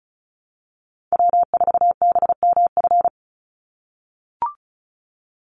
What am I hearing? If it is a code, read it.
Morse code, heard as a keyed tone, about 34 wpm, 700 Hz. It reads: W4BMF
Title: USA Call Signs 35wpm